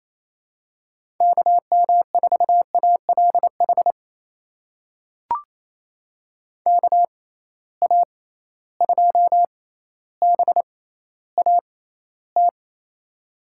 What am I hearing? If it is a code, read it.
Morse code, heard as a keyed tone, about 28 wpm, spaced as for 14 wpm, 700 Hz. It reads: KM4ALH KA2BAT
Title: USA Call Signs - 6 Characters - 6x Character Spacing - 28wpm